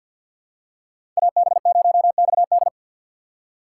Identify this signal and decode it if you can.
Morse code, heard as a keyed tone, about 51 wpm, 700 Hz. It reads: AB0XD
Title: USA Call Signs - 2x Character Spacing - 50wpm